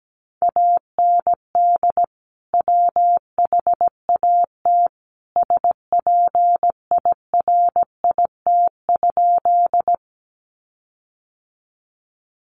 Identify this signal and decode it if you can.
Morse code, heard as a keyed tone, about 17 wpm, 700 Hz. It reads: AND WHAT SPIRIT?
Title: Sentences from Top 1400 Words 17wpm